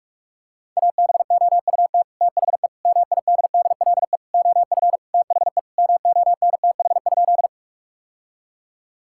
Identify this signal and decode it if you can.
Morse code, heard as a keyed tone, about 42 wpm, 700 Hz. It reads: ABOUT THE MIDDLE OF THE MONTH?
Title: Sentences from Top 1000 Words - Review Part 6 - 45wpm